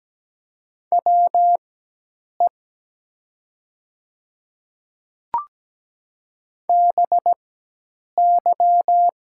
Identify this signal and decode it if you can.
Morse code, heard as a keyed tone, about 17 wpm, spaced as for 8 wpm, 700 Hz. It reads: WE BY
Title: 4x Character Spacing - Top 100 Words - 17wpm